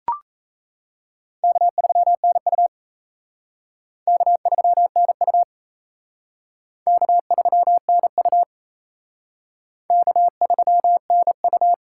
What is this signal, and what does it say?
Morse code, heard as a keyed tone, about 34 wpm, 700 Hz. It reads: K3NU K3NU K3NU K3NU
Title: Speed Racing - USA 1x2 Call Signs 28wpm